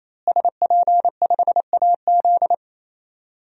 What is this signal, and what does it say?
Morse code, heard as a keyed tone, about 28 wpm, 700 Hz. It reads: SP5AZ
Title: DXCC-100 Call Signs - 5 Characters - Repeated 3x - 28wpm